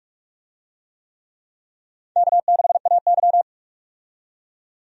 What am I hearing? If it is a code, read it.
Morse code, heard as a keyed tone, about 46 wpm, 700 Hz. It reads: K6AY
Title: USA Call Signs - 4 Characters - 45wpm